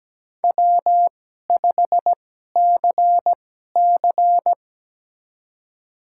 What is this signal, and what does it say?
Morse code, heard as a keyed tone, about 17 wpm, spaced as for 12 wpm, 700 Hz. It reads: W5CC
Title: USA Call Signs - 4 Characters - 2x Character Spacing - 17wpm